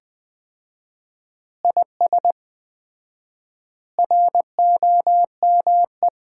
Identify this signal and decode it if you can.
Morse code, heard as a keyed tone, about 21 wpm, 700 Hz. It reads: IS ROME
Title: Common QSO Phrases - 4x Word-Spacing - 20wpm